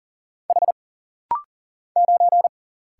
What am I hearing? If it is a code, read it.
Morse code, heard as a keyed tone, about 40 wpm, 700 Hz. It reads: H 9
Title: Single Letter-Number - Rapid-Fire 40wpm